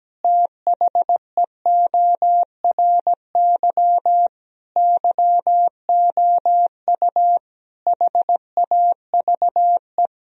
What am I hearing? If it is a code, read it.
Morse code, heard as a keyed tone, about 17 wpm, 700 Hz. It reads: THEORY YOU HAVE
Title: Sentences from Top 1000 Words - Review Part 3 - 17wpm